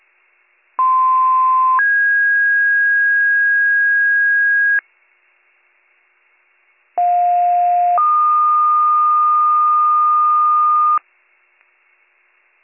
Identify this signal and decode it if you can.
Two-tone page, heard as a one-second tone, then a three-second tone, about 1040 then 1650 Hz, about 710 then 1220 Hz. It is heard from 0.8 to 11.0 s.